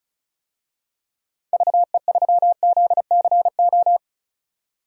Morse code, heard as a keyed tone, about 34 wpm, 700 Hz. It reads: VE3ZCO
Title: DXCC-100 Call Signs - 6 Characters - 35wpm